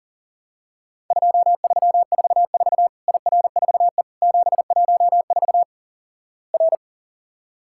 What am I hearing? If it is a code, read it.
Morse code, heard as a keyed tone, about 38 wpm, 700 Hz. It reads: 2344 IR4E 714 R
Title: Worked All Europe DX Contest - QTC Exchange - 40wpm